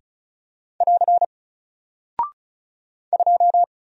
Morse code, heard as a keyed tone, about 36 wpm, 700 Hz. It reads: <AR> 2
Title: Single Character (Letter:Number:Punctuation:Prosign) 35wpm